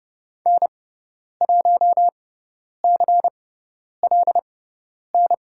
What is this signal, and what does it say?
Morse code, heard as a keyed tone, about 30 wpm, spaced as for 10 wpm, 700 Hz. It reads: N1CLN
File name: Lesson 16g - Letter c - Call Signs 30wpm